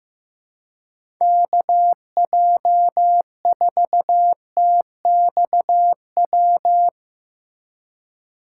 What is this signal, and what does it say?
Morse code, heard as a keyed tone, about 15 wpm, 700 Hz. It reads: KJ4TXW
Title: USA Call Signs - 6 Characters - 4x Character Spacing - 15wpm